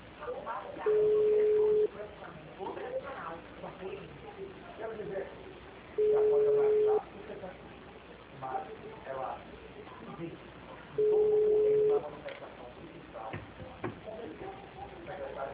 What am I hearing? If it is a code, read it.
Ringback tone (Europe, 425 Hz), with speech.
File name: out- -2054- - - .166